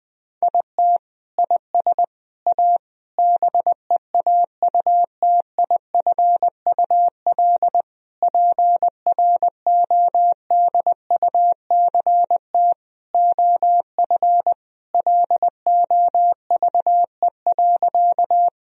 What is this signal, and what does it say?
Morse code, heard as a keyed tone, about 20 wpm, 700 Hz. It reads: IT IS A BEAUTIFUL PRODUCT OF LOVE.
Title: Sentences from Top 1000 Words - Review Part 2 - 20wpm